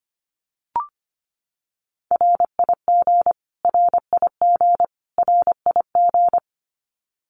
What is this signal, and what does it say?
Morse code, heard as a keyed tone, about 25 wpm, 700 Hz. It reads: RIG RIG RIG
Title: Abbreviations Repeated 3x - 25wpm